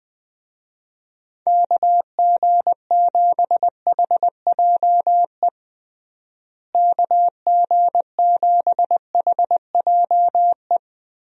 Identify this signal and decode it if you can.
Morse code, heard as a keyed tone, about 21 wpm, 700 Hz. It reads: KG7HJE KG7HJE
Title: USA Call Signs - 6 Characters - Repeated 3x - 20wpm